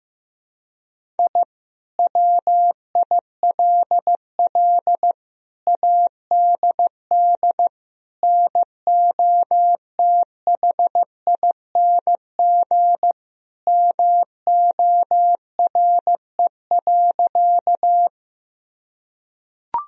Morse code, heard as a keyed tone, about 15 wpm, 700 Hz. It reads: I WILL ADD NOTHING MORE.
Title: Sentences from Top 1000 Words - Review Part 1 - 15wpm